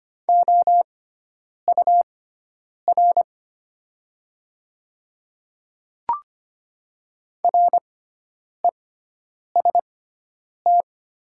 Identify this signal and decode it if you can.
Morse code, heard as a keyed tone, about 25 wpm, spaced as for 9 wpm, 700 Hz. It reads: OUR REST
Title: 6x Character Spacing - Top 500 Words - 25wpm